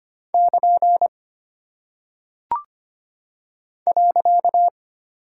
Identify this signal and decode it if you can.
Morse code, heard as a keyed tone, about 25 wpm, 700 Hz. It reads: <KN> .
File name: practice / 25wpm / Single Punctuation-Prosign - 25wpm